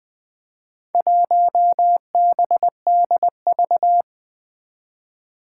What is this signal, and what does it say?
Morse code, heard as a keyed tone, about 20 wpm, 700 Hz. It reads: 1BDV
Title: Four Challenging Letter-Number Combinations 20wpm